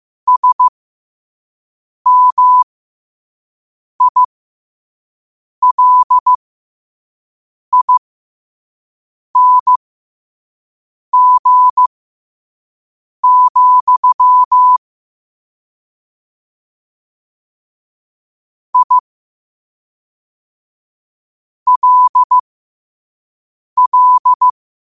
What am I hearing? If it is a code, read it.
Morse code, heard as a keyed tone, about 15 wpm, spaced as for 5 wpm, 1000 Hz. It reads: SMILING, I LL
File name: qotdfiles / 5wpm